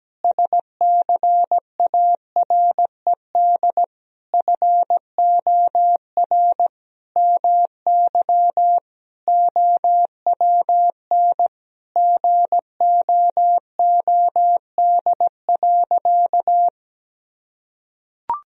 Morse code, heard as a keyed tone, about 17 wpm, 700 Hz. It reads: SCARED FOR MY OWN GOOD.
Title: Sentences from Top 1000 Words - Review Part 4 - 17wpm